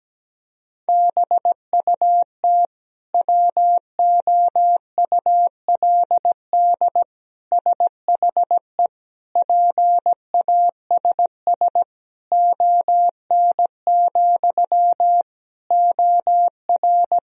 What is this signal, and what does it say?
Morse code, heard as a keyed tone, about 17 wpm, 700 Hz. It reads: BUT WOULD SHE PASS ON, OR
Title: Sentences from Top 1000 Words - Review Part 1 - 17wpm